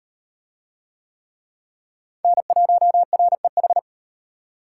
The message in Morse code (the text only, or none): N1REH